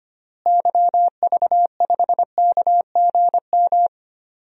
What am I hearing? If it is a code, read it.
Morse code, heard as a keyed tone, about 25 wpm, 700 Hz. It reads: YV5KGM
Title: DXCC-100 Call Signs - 6 Characters - Repeated 3x - 25wpm